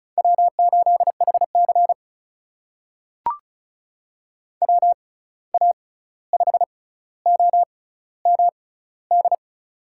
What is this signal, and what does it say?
Morse code, heard as a keyed tone, about 36 wpm, spaced as for 16 wpm, 700 Hz. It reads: W8HC WA5OMD
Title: USA Call Signs - 6x Character Spacing - 35wpm